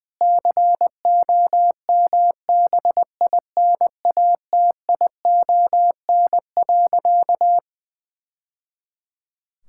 Morse code, heard as a keyed tone, about 21 wpm, 700 Hz. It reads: COMBINATION.
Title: Sentences from Top 1000 Words - Review Part 6 - 20wpm